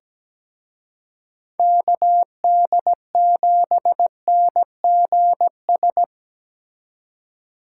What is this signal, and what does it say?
Morse code, heard as a keyed tone, about 20 wpm, 700 Hz. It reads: KD7NGS
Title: Speed Racing - USA 2x3 Call Signs 17wpm